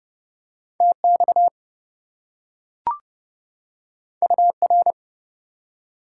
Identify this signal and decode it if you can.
Morse code, heard as a keyed tone, about 32 wpm, 700 Hz. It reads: TX UR
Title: Top 90 QSO Elements 30wpm